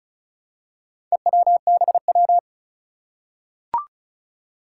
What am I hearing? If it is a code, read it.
Morse code, heard as a keyed tone, about 35 wpm, 700 Hz. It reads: EWBW
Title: Four Challenging Letter-Number Combinations 35wpm